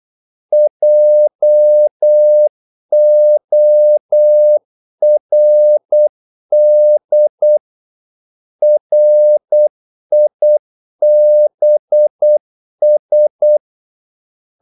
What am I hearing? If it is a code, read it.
Morse code, heard as a keyed tone, about 8 wpm, 600 Hz. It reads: JORD RIBS
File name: Korte ord DK 8wpm